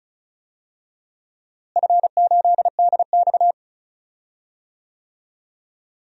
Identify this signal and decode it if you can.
Morse code, heard as a keyed tone, about 36 wpm, 700 Hz. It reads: F8DX
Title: DXCC-100 Call Signs - 4 Characters - Repeated 2x - 35wpm